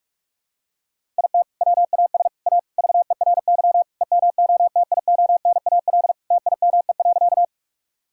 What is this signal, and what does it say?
Morse code, heard as a keyed tone, about 41 wpm, 700 Hz. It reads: IT WAS A VERY EMOTIONAL TIME.
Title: Sentences from Top 1000 Words - Review Part 5 - 45wpm